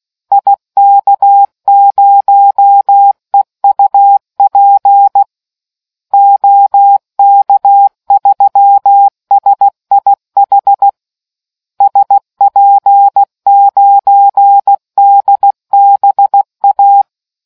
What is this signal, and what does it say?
Morse code, heard as a keyed tone, about 16 wpm, 800 Hz. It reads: IK0EUP OK3SIH SP9DBA